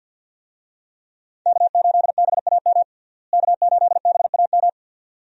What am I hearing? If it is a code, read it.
Morse code, heard as a keyed tone, about 49 wpm, 700 Hz. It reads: K8BAM K8BAM
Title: USA Call Signs - 5 Characters - Repeated 2x - 50wpm